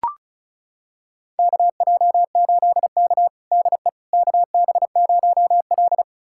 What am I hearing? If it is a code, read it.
Morse code, heard as a keyed tone, about 36 wpm, 690 Hz. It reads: KJ8K DE KB0L
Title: USA Callsign DE Callsign - 4 Characters - 35wpm